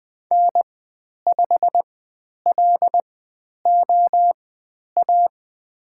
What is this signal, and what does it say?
Morse code, heard as a keyed tone, about 20 wpm, spaced as for 10 wpm, 700 Hz. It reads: N5LOA
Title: 102 Lesson 17g - Letter u - Call Signs 20wpm